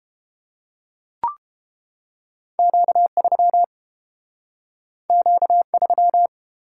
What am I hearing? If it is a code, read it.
Morse code, heard as a keyed tone, about 31 wpm, 700 Hz. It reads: Q3 Q3